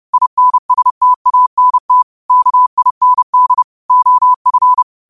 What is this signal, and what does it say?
Morse code, heard as a keyed tone, about 30 wpm, 1000 Hz. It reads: INSTANT KIND OF
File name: qotdfiles / 30wpm